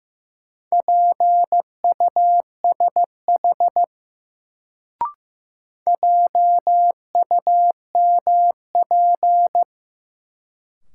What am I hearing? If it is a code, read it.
Morse code, heard as a keyed tone, about 15 wpm, 700 Hz. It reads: PUSH JUMP
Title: Sentences from Top 1300 Words - New Word Drill - 15wpm